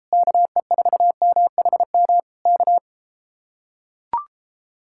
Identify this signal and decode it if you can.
Morse code, heard as a keyed tone, about 33 wpm, 700 Hz. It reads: KE4MHM K